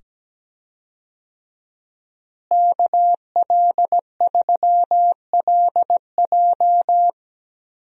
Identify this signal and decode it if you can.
Morse code, heard as a keyed tone, about 17 wpm, 700 Hz. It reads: KL3LJ
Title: USA Call Signs - 5 Characters - Repeated 2x - 17wpm